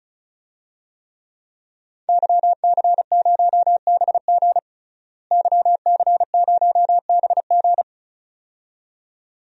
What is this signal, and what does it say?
Morse code, heard as a keyed tone, about 36 wpm, 700 Hz. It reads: YC0BG YC0BG
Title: DXCC-100 Call Signs - 5 Characters - Repeated 2x - 35wpm